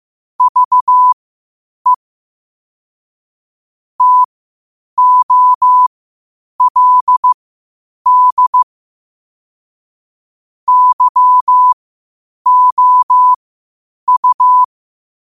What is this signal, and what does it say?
Morse code, heard as a keyed tone, about 15 wpm, spaced as for 8 wpm, 1000 Hz. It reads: VE TOLD YOU
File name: 8wpm